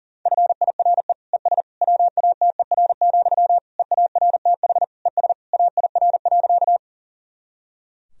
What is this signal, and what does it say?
Morse code, heard as a keyed tone, about 40 wpm, 700 Hz. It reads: FIRE ES WATER, EARTH ES AIR.
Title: Sentences from Top 400 Words and Abbreviations 40wpm